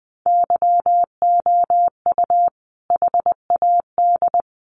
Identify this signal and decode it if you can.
Morse code, heard as a keyed tone, about 20 wpm, 700 Hz. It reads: YOU HAD
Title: Sentences from Top 400 Words 20wpm